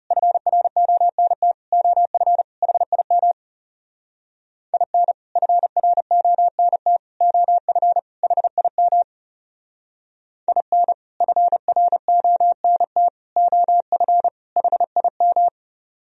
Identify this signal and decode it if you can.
Morse code, heard as a keyed tone, about 38 wpm, 700 Hz. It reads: FRONT OF HIM IN FRONT OF HIM IN FRONT OF HIM